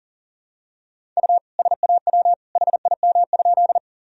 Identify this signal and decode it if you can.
Morse code, heard as a keyed tone, about 40 wpm, 700 Hz. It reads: U SAW HIM?
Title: Sentences from Top 300 Words and Abbreviations 40wpm